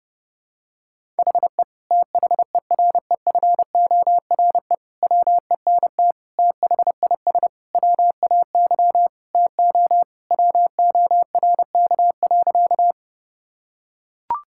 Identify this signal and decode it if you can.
Morse code, heard as a keyed tone, about 32 wpm, 700 Hz. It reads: HE THEREFORE WENT THIS WAY TO WORK.
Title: Sentences from Top 1000 Words - Review Part 5 - 30wpm